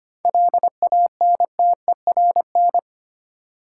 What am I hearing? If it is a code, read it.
Morse code, heard as a keyed tone, about 25 wpm, 690 Hz. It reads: LANTERN
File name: practice / 25wpm / Easy Words with Hints - 25wpm